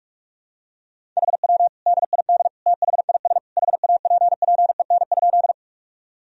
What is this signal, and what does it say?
Morse code, heard as a keyed tone, about 46 wpm, 700 Hz. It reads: HW DID THIS HAPPEN?
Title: Sentences from Top 400 Words and Abbreviations 45wpm